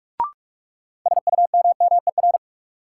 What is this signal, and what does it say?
Morse code, heard as a keyed tone, about 46 wpm, 700 Hz. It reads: SUMMER
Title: Sentences from Top 1100 Words - New Word Drill Alt - 45wpm